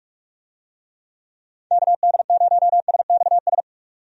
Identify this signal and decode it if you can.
Morse code, heard as a keyed tone, about 46 wpm, 700 Hz. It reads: KD0SXS
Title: USA Call Signs - 6 Characters - 45wpm